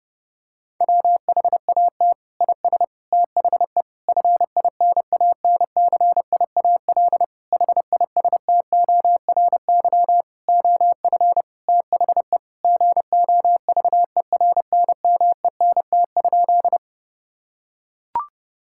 Morse code, heard as a keyed tone, about 30 wpm, 700 Hz. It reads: WHAT IS THE FINANCIAL HISTORY OF THE GOVERNMENT?
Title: Sentences from Top 1000 Words - Review Part 5 - 30wpm